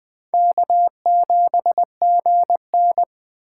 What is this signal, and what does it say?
Morse code, heard as a keyed tone, about 23 wpm, 700 Hz. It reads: K7GN
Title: Speed Racing - USA 1x2 Call Signs 20wpm